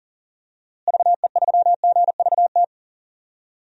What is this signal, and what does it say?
Morse code, heard as a keyed tone, about 40 wpm, 700 Hz. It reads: VE3GVT
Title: DXCC-100 Call Signs - 6 Characters - Repeated 3x - 40wpm